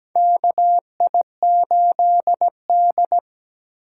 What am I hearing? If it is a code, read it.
Morse code, heard as a keyed tone, about 17 wpm, 700 Hz. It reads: KI8D
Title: USA Call Signs - 4 Characters - 3x Character Spacing - 17wpm